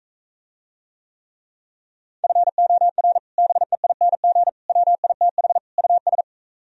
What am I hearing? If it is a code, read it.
Morse code, heard as a keyed tone, about 45 wpm, 700 Hz. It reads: FOR BEING WITH US